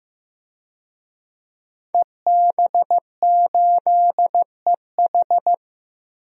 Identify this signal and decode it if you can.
Morse code, heard as a keyed tone, about 15 wpm, 700 Hz. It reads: EB8EH
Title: DXCC-100 Call Signs - 5 Characters - 15wpm